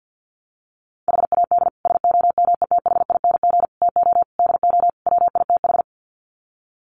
Morse code, heard as a keyed tone, about 50 wpm, 700 Hz. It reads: HAD SOMETHING TO DO WITH
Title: Sets of 5 Words 50wpm